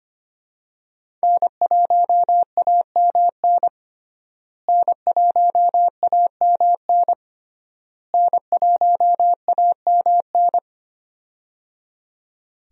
Morse code, heard as a keyed tone, about 25 wpm, 700 Hz. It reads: N1AMN N1AMN N1AMN
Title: USA Call Signs - 5 Characters - Repeated 3x - 25wpm